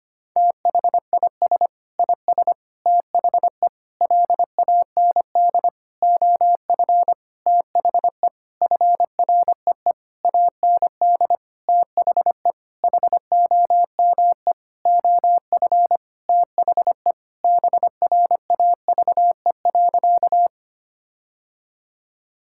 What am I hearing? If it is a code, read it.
Morse code, heard as a keyed tone, about 25 wpm, 700 Hz. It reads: THIS IS THE LAND OF THE FREE AND THE HOME OF THE BRAVE.
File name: practice / 25wpm / Sentences with 9-Word Phrases - 25wpm